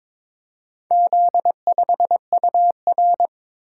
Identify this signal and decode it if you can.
Morse code, heard as a keyed tone, about 22 wpm, 700 Hz. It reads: Z5UR
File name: practice / 22wpm / Four Letter-Number Combinations - 22wpm